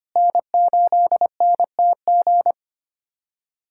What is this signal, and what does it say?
Morse code, heard as a keyed tone, about 25 wpm, 700 Hz. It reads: N8NTG
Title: USA Callsign DE Callsign - 5 Characters - 25wpm